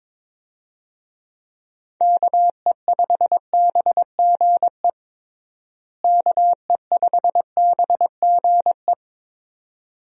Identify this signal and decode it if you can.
Morse code, heard as a keyed tone, about 22 wpm, 700 Hz. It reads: KE5BGE KE5BGE
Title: USA Call Signs - 6 Characters - Repeated 2x - 22wpm